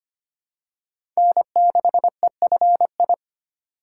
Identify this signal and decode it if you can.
Morse code, heard as a keyed tone, about 25 wpm, 700 Hz. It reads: N6EFI
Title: USA Call Signs - 5 Characters - 2x Character Spacing - 25wpm